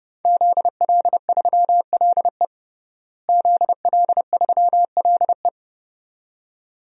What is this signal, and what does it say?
Morse code, heard as a keyed tone, about 30 wpm, 700 Hz. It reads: ZL3LE ZL3LE
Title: DXCC-100 Call Signs - 5 Characters - Repeated 2x - 30wpm